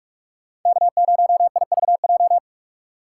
Morse code, heard as a keyed tone, about 59 wpm, 700 Hz. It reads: K0IVJ